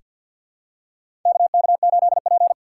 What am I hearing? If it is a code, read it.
Morse code, heard as a keyed tone, about 48 wpm, 700 Hz. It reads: KK8P
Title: USA Call Signs - 4 Characters - 3x Character Spacing - 50wpm